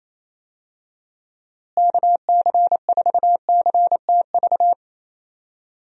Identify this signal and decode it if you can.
Morse code, heard as a keyed tone, about 28 wpm, 700 Hz. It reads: KC4CTV
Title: USA Call Signs - 6 Characters - Repeated 2x - 28wpm